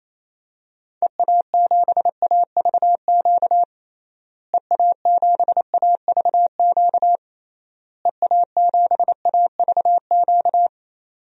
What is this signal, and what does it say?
Morse code, heard as a keyed tone, about 28 wpm, 700 Hz. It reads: EA7AVQ EA7AVQ EA7AVQ
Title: DXCC-100 Call Signs - 6 Characters - Repeated 3x - 28wpm